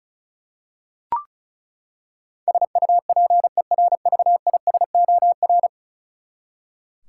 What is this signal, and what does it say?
Morse code, heard as a keyed tone, about 36 wpm, 700 Hz. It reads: SUPERVISOR
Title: Practice QSO - Exchange 3 - More Info - 35wpm